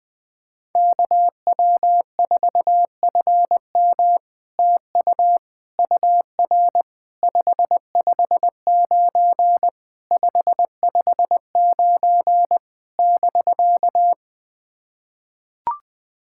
Morse code, heard as a keyed tone, about 20 wpm, 700 Hz. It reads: KW4FM TU UR 559 559 <BK>
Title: POTA QSO Practice - 20wpm